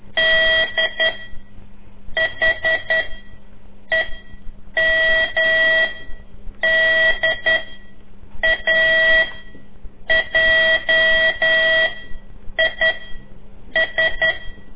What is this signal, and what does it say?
Morse code, heard as a keyed tone, about 10 wpm, 1950 Hz. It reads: D H E M D A J I S
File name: CORSI CW